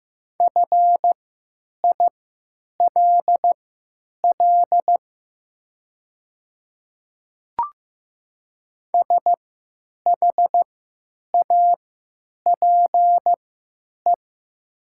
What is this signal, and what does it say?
Morse code, heard as a keyed tone, about 15 wpm, spaced as for 8 wpm, 700 Hz. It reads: FILL SHAPE
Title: 3x Character Spacing - Top 500 Words - 15wpm